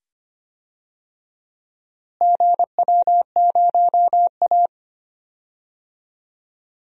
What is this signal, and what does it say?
Morse code, heard as a keyed tone, about 25 wpm, 700 Hz. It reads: GW0A
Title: DXCC-100 Call Signs - 4 Characters - Repeated 2x - 25wpm